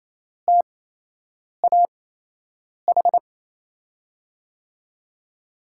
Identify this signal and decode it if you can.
Morse code, heard as a keyed tone, about 28 wpm, spaced as for 8 wpm, 700 Hz. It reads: TAH
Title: USA Call Signs - 5 Characters - 8x Character Spacing - 28wpm